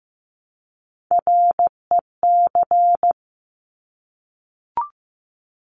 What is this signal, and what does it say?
Morse code, heard as a keyed tone, about 15 wpm, 700 Hz. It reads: REC
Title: Top Three Letter Combinations in Words 15wpm